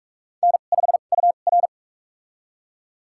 Morse code, heard as a keyed tone, about 46 wpm, spaced as for 33 wpm, 700 Hz. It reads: N5UR
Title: USA Call Signs - 4 Characters - 2x Character Spacing - 45wpm